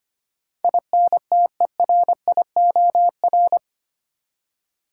Morse code, heard as a keyed tone, about 25 wpm, 700 Hz. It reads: INTERIOR
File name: Eight Letter Words - 25wpm